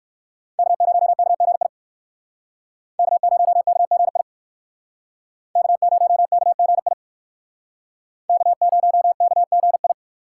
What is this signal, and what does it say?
Morse code, heard as a keyed tone, about 62 wpm, 700 Hz. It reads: K0KGI K0KGI K0KGI K0KGI